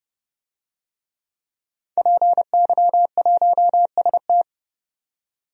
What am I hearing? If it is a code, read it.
Morse code, heard as a keyed tone, about 32 wpm, 700 Hz. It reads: PY1ST
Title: DXCC-100 Call Signs - 5 Characters - 30wpm